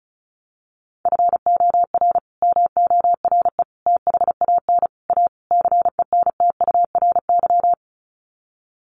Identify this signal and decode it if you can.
Morse code, heard as a keyed tone, about 34 wpm, 700 Hz. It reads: FOR MORE THAN A CENTURY
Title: Sets of 5 Words Encore2 35wpm